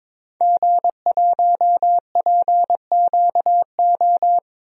Morse code, heard as a keyed tone, about 22 wpm, 700 Hz. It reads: G1PQO
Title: DXCC-100 Call Signs - 5 Characters - Repeated 3x - 22wpm